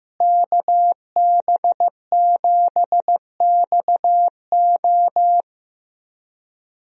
Morse code, heard as a keyed tone, about 15 wpm, 700 Hz. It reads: KB7XO
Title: USA Call Signs - 5 Characters - Repeated 3x - 15wpm